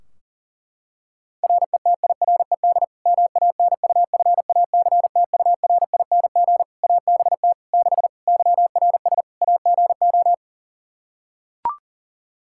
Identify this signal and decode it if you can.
Morse code, heard as a keyed tone, about 40 wpm, 700 Hz. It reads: RETIRED MANUFACTURING ABT 6 YRS AGO
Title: Practice QSO - Exchange 3 - More Info - 40wpm